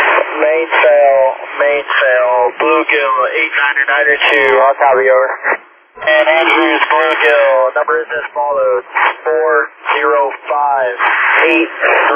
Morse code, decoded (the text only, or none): none